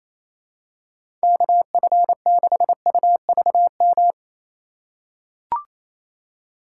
Morse code, heard as a keyed tone, about 28 wpm, 700 Hz. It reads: KF6UVM
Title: USA Call Signs - 6 Characters - 3x Character Spacing - 28wpm